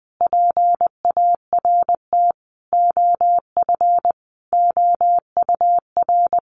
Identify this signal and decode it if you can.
Morse code, heard as a keyed tone, about 20 wpm, 700 Hz. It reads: PART OF OUR
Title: Sentences from Top 1400 Words 20wpm